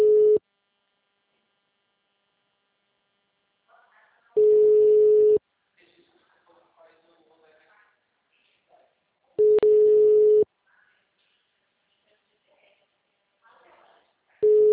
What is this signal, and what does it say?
Ringback tone (Europe, 425 Hz), with speech.
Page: Aracatuba - SP